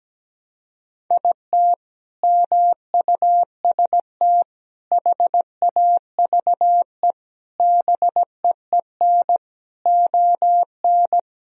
Morse code, heard as a keyed tone, about 17 wpm, 700 Hz. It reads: IT MUST HAVE BEEN ON
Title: Sentences from Top 1000 Words - Review Part 1 - 17wpm